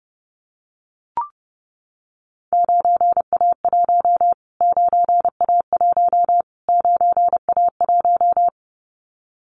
Morse code, heard as a keyed tone, about 28 wpm, 700 Hz. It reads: 9A1 9A1 9A1
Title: Callsign Prefixes for Top 25 DXCC - Repeated 3x - 30wpm